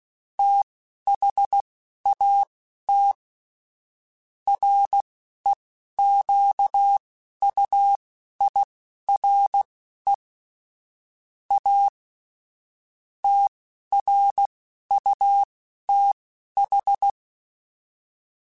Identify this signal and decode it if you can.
Morse code, heard as a keyed tone, about 16 wpm, spaced as for 11 wpm, 800 Hz. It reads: THAT REQUIRE A TRUTH
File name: Technical Subjects / Technical Subjects-mp3s / AND Gate @10